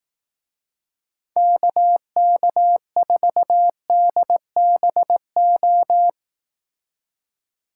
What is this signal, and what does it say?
Morse code, heard as a keyed tone, about 18 wpm, 700 Hz. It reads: KK4DBO
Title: Speed Racing - USA 2x3 Call Signs 15wpm